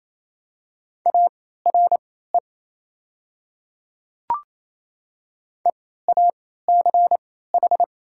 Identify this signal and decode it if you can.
Morse code, heard as a keyed tone, about 28 wpm, spaced as for 16 wpm, 700 Hz. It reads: ARE EACH
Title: 3x Character Spacing - Top 100 Words - 28wpm